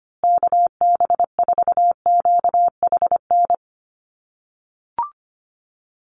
Morse code, heard as a keyed tone, about 25 wpm, 700 Hz. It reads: KB4QHN
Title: USA Call Signs Repeated 3x 25wpm